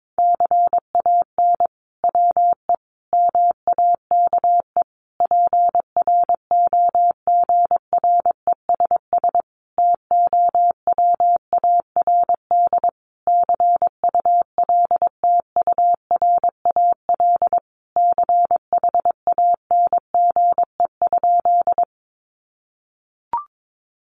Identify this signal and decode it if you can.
Morse code, heard as a keyed tone, about 22 wpm, 700 Hz. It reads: CAN WE MAKE PROGRESS TOWARD CULTURAL CHANGE?
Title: Sentences from Top 1500 Words 22wpm